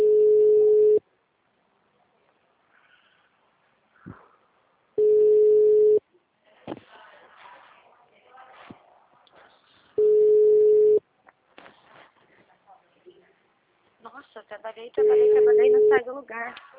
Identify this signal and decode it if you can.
Ringback tone (Europe, 425 Hz), with speech.
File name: out- -2023- - - .10306